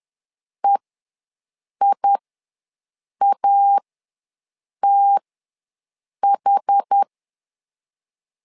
Morse code, heard as a keyed tone, about 10 wpm, spaced as for 6 wpm, 800 Hz. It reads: EIATH
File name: LEZIONI / 2 UHDN / 2 UHD ws 10wpm 2058